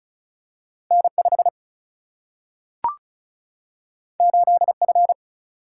Morse code, heard as a keyed tone, about 36 wpm, 700 Hz. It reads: N 5 8 F
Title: Single-Letter Single-Number 35wpm